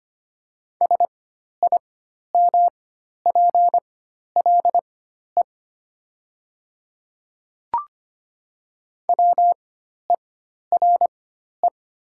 Morse code, heard as a keyed tone, about 25 wpm, spaced as for 12 wpm, 700 Hz. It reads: SIMPLE WERE
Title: 4x Character Spacing - Top 500 Words - 25wpm